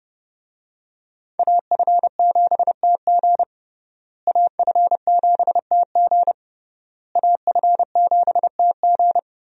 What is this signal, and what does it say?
Morse code, heard as a keyed tone, about 30 wpm, 700 Hz. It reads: AF7TG AF7TG AF7TG
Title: USA Call Signs - 5 Characters - Repeated 3x - 30wpm